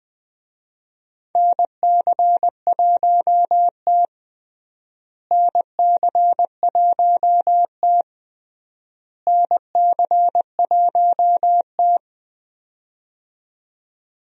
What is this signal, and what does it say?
Morse code, heard as a keyed tone, about 20 wpm, 700 Hz. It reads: NC1T NC1T NC1T
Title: USA Call Signs - 4 Characters - Repeated 3x - 20wpm